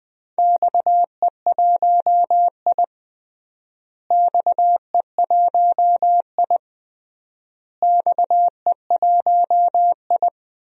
Morse code, heard as a keyed tone, about 21 wpm, 700 Hz. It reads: XE1I XE1I XE1I
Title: DXCC-100 Call Signs - 4 Characters - Repeated 3x - 20wpm